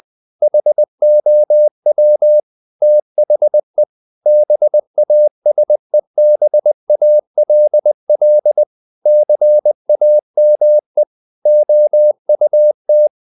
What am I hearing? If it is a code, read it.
Morse code, heard as a keyed tone, about 20 wpm, 600 Hz. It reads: HOW THE BASEBALL CAME OUT
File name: the old man and the sea / the old man and the sea part 1 20wpm